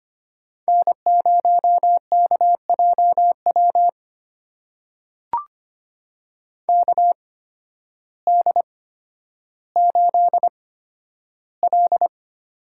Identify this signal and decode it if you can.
Morse code, heard as a keyed tone, about 25 wpm, spaced as for 12 wpm, 700 Hz. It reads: N0KJW KD8L
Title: USA Call Signs - 8x Character Spacing - 25wpm